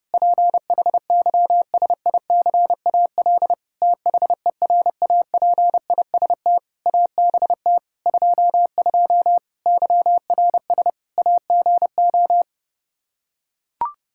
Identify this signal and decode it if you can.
Morse code, heard as a keyed tone, about 32 wpm, 700 Hz. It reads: PHYSICAL THERAPIST ABT 22 YRS AGO
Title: Practice QSO - Exchange 3 - More Info - 30wpm